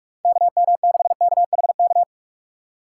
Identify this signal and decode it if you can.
Morse code, heard as a keyed tone, about 44 wpm, 700 Hz. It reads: KM6KHK